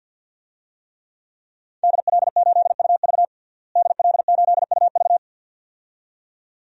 Morse code, heard as a keyed tone, about 49 wpm, 700 Hz. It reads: DL8UV DL8UV
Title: DXCC-100 Call Signs - 5 Characters - Repeated 2x - 50wpm